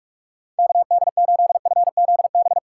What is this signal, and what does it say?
Morse code, heard as a keyed tone, about 46 wpm, 700 Hz. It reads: KD8FZB
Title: USA Call Signs - 6 Characters - 2x Character Spacing - 45wpm